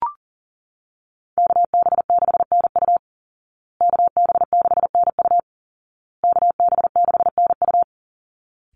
Morse code, heard as a keyed tone, about 40 wpm, 700 Hz. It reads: KB6NU KB6NU KB6NU
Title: Call Signs Repeated 2x - 40wpm